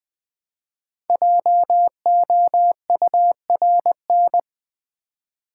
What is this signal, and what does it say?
Morse code, heard as a keyed tone, about 20 wpm, 700 Hz. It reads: JOURN
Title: Build A Word - 5 Letters At A Time - 10-Letter Words - 20wpm